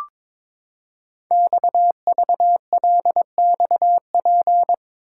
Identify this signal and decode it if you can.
Morse code, heard as a keyed tone, about 22 wpm, 700 Hz. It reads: XVLXP
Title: Code Groups - 5 Letters - 22wpm